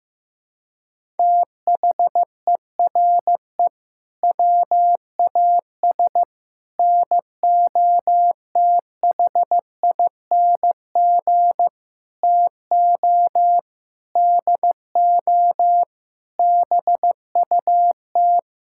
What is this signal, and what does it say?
Morse code, heard as a keyed tone, about 15 wpm, 700 Hz. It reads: THERE WAS NOTHING TO DO BUT
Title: Sentences from Top 1000 Words - Review Part 6 - 15wpm